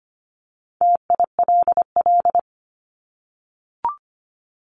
Morse code, heard as a keyed tone, about 25 wpm, 700 Hz. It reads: TILL
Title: Top 1751-2000 Words 25wpm